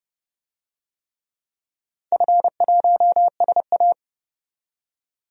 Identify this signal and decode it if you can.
Morse code, heard as a keyed tone, about 30 wpm, 700 Hz. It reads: F1SA
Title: DXCC-100 Call Signs - 4 Characters - 30wpm